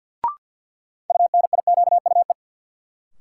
Morse code, heard as a keyed tone, about 50 wpm, 700 Hz. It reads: UNIQUE
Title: Sentences from Top 1000 Words - Review Part 7 - New Word Drill - 50wpm